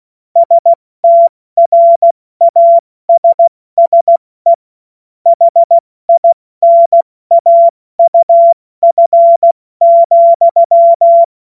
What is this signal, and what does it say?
Morse code, heard as a keyed tone, about 16 wpm, 680 Hz. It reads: STRASSE HINAUF,
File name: Morsekurs / Witze / Witz 14. Dezember 2021, 70 F80